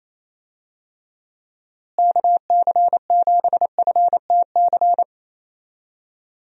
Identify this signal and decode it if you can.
Morse code, heard as a keyed tone, about 28 wpm, 700 Hz. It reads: KC7FTC
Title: USA Call Signs - 6 Characters - Repeated 3x - 28wpm